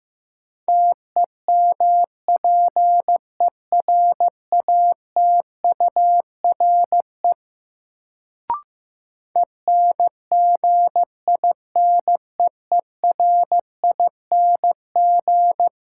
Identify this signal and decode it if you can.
Morse code, heard as a keyed tone, about 15 wpm, 700 Hz. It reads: TEMPERATURE ENGINEERING
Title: Sentences from Top 1500 Words - New Word Drill Alt - 15wpm